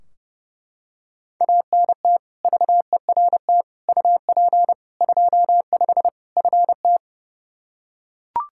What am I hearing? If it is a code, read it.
Morse code, heard as a keyed tone, about 30 wpm, 700 Hz. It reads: ANT VERT UP 25 FT
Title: K1USN SST QSO Practice 30wpm